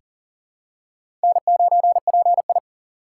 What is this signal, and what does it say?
Morse code, heard as a keyed tone, about 40 wpm, 700 Hz. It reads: N9PI
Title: USA Call Signs - 6x Character Spacing - 40wpm